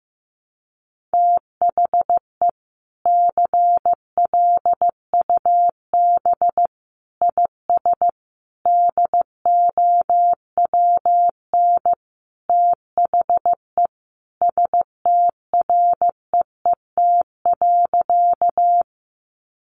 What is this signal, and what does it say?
Morse code, heard as a keyed tone, about 15 wpm, 700 Hz. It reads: THE CLUB IS DOWN THE STREET.
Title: Sentences from Top 1200 Words 15wpm